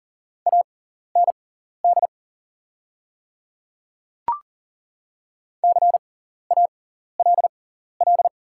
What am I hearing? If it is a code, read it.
Morse code, heard as a keyed tone, about 38 wpm, spaced as for 14 wpm, 690 Hz. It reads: AND CALL
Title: 6x Character Spacing - Top 100 Words - 40wpm